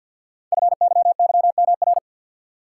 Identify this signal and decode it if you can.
Morse code, heard as a keyed tone, about 49 wpm, 700 Hz. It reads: FYYMR
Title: Code Groups - 5 Letter:Numbers - 50wpm